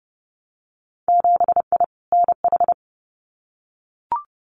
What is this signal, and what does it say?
Morse code, heard as a keyed tone, about 30 wpm, 700 Hz. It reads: 7I NH
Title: Winter Field Day Exchanges Encore 30wpm